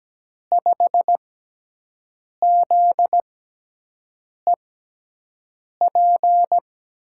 Morse code, heard as a keyed tone, about 17 wpm, spaced as for 6 wpm, 700 Hz. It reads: 5ZEP
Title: USA Call Signs - 6x Character Spacing - 17wpm